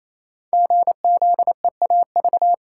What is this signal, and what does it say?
Morse code, heard as a keyed tone, about 28 wpm, 700 Hz. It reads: GZEAV
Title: Code Groups - 5 Letter:Numbers - 28wpm